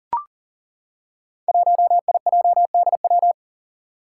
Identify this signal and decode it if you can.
Morse code, heard as a keyed tone, about 40 wpm, 700 Hz. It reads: 1IJDW
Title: Five Challenging Letter-Number Combinations 40wpm